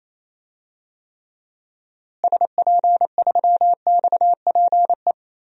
Morse code, heard as a keyed tone, about 28 wpm, 700 Hz. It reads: SP3XPE
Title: DXCC-100 Call Signs - 6 Characters - Repeated 2x - 28wpm